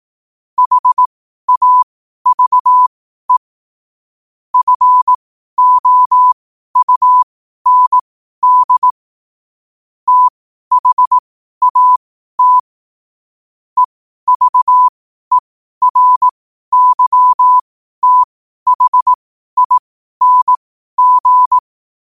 Morse code, heard as a keyed tone, about 18 wpm, spaced as for 12 wpm, 1000 Hz. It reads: HAVE FOUND THAT EVERYTHING
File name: qotdfiles / 12wpm